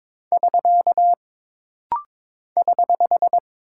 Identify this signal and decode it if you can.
Morse code, heard as a keyed tone, about 22 wpm, 700 Hz. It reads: <SK> <HH>
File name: practice / 22wpm / Single Punctuation-Prosign - Rapid-Fire - 22wpm